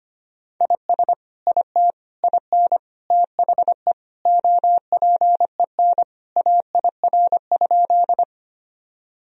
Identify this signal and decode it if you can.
Morse code, heard as a keyed tone, about 25 wpm, 700 Hz. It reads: IS IT IN THE OPEN AIR?
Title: Sentences from Top 1000 Words - Review Part 1 - 25wpm